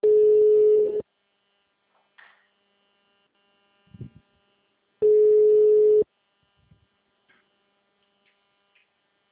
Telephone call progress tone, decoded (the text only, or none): ringback tone